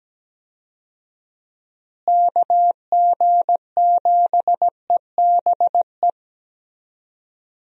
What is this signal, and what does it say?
Morse code, heard as a keyed tone, about 17 wpm, 700 Hz. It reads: KG7EBE
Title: USA Call Signs - 6 Characters - Repeated 3x - 17wpm